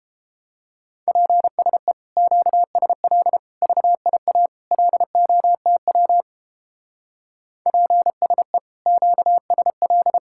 Morse code, heard as a keyed tone, about 31 wpm, 700 Hz. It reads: PSE QSL VIA LOTW PSE QSL